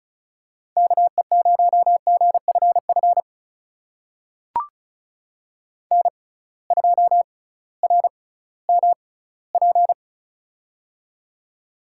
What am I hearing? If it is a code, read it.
Morse code, heard as a keyed tone, about 36 wpm, spaced as for 19 wpm, 700 Hz. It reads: KE0GFF N2RMP
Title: USA Call Signs - 6x Character Spacing - 35wpm